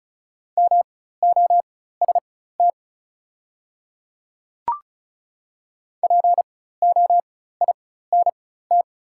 Morse code, heard as a keyed tone, about 36 wpm, spaced as for 16 wpm, 700 Hz. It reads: MOST POINT
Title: 4x Character Spacing - Top 200 Words - 35wpm